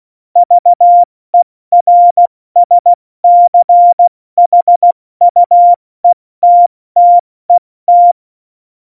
Morse code, heard as a keyed tone, about 16 wpm, 700 Hz. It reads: VERSCHUETTET
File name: Morsekurs / Witze / Witz 28. Januar 2020, 70 F80